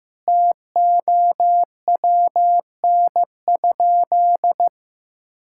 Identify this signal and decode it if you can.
Morse code, heard as a keyed tone, about 15 wpm, 700 Hz. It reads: TOWN?
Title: Sentences from Top 1000 Words - Review Part 6 - 15wpm